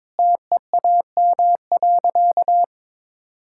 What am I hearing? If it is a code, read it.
Morse code, heard as a keyed tone, about 22 wpm, 700 Hz. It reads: TEAM.
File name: practice / 22wpm / Sentences with 3-Word Phrases - 22wpm